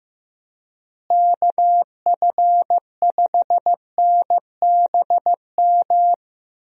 Morse code, heard as a keyed tone, about 15 wpm, 700 Hz. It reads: KF5NBM
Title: USA Call Signs - 6 Characters - Repeated 2x - 15wpm